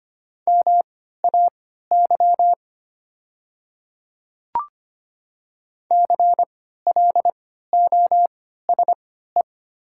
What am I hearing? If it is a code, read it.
Morse code, heard as a keyed tone, about 25 wpm, spaced as for 14 wpm, 700 Hz. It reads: MAY CLOSE
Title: 3x Character Spacing - Top 300 Words - 25wpm